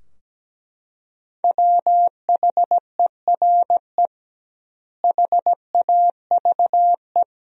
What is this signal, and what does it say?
Morse code, heard as a keyed tone, about 17 wpm, 700 Hz. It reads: WHERE HAVE
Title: 2x Word-Spacing - Sentences from Top 200 Words - 17wpm